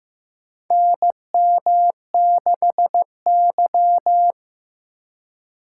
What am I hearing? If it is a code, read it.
Morse code, heard as a keyed tone, about 15 wpm, 700 Hz. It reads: NM6Y
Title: USA Call Signs - 4 Characters - 3x Character Spacing - 15wpm